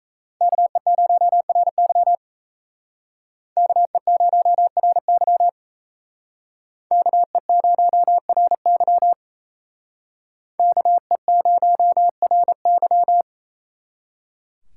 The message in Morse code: KE0RY KE0RY KE0RY KE0RY

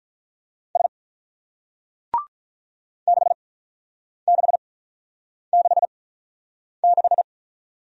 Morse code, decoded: S 6666

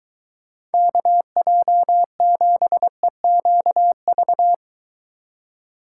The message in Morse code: KJ7EQV

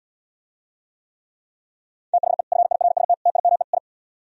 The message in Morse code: THE FIRST TIME I

